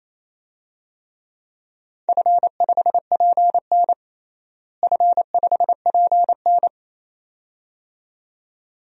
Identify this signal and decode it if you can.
Morse code, heard as a keyed tone, about 28 wpm, 700 Hz. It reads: F5PN F5PN
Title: DXCC-100 Call Signs - 4 Characters - Repeated 2x - 28wpm